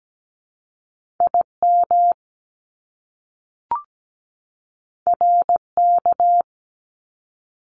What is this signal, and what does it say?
Morse code, heard as a keyed tone, about 17 wpm, 700 Hz. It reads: IM RK
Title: Top Two Letter Combinations in Words 17wpm